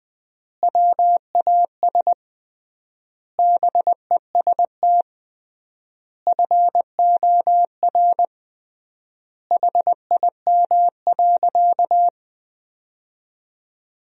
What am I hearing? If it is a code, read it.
Morse code, heard as a keyed tone, about 20 wpm, 700 Hz. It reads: WAS BEST FOR HIM.
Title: 3x Word-Spacing - Sentences from Top 500 Words - 20wpm